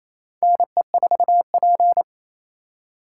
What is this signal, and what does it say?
Morse code, heard as a keyed tone, about 28 wpm, 700 Hz. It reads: NE4P
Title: USA Call Signs - 4 Characters - 4x Character Spacing - 28wpm